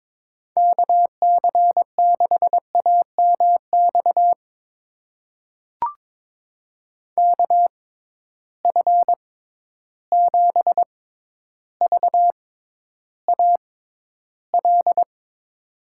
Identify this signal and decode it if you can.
Morse code, heard as a keyed tone, about 22 wpm, spaced as for 11 wpm, 700 Hz. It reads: KC6AMX KF7VAL
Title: USA Call Signs - 6x Character Spacing - 22wpm